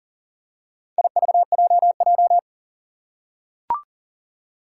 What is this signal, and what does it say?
Morse code, heard as a keyed tone, about 40 wpm, 690 Hz. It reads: IVJJ
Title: Four Challenging Letter-Number Combinations 40wpm